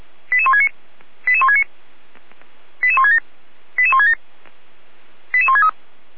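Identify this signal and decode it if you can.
Five-tone selective call (ZVEI), heard as quick runs of five tones, tones of about 70 ms. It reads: 88169 88169 88167 88167 88163
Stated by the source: ZVEI five-tone selective call received on an SDR.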